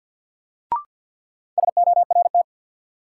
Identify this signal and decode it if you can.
Morse code, heard as a keyed tone, about 53 wpm, 700 Hz. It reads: SORT
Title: Sentences from Top 1300 Words - New Word Drill Alt - 50wpm